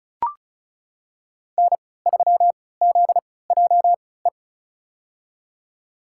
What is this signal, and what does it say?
Morse code, heard as a keyed tone, about 36 wpm, spaced as for 20 wpm, 700 Hz. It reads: N3ZJE
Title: USA Call Signs - 5 Characters - 3x Character Spacing - 35wpm